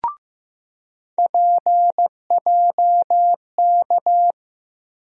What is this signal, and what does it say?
Morse code, heard as a keyed tone, about 15 wpm, 700 Hz. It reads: PJK
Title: USA Call Sign Postfixes - 3 Letters 15wpm